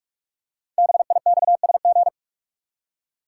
Morse code, heard as a keyed tone, about 44 wpm, 700 Hz. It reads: BIXSG
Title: Code Groups - 5 Letters - 45wpm